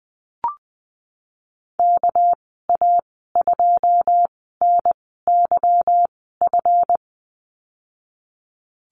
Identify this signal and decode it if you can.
Morse code, heard as a keyed tone, about 20 wpm, spaced as for 14 wpm, 700 Hz. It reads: KA2NYF
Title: 207 Lesson 33g - Letter j - Call Signs 20wpm